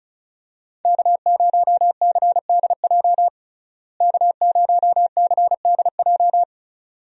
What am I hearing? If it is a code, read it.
Morse code, heard as a keyed tone, about 36 wpm, 700 Hz. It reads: K0CDJ K0CDJ
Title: USA Call Signs - 5 Characters - Repeated 2x - 35wpm